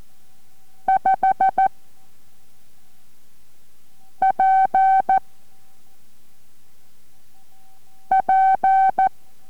Morse code, heard as a keyed tone, about 14 wpm, 770 Hz. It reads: 5 P P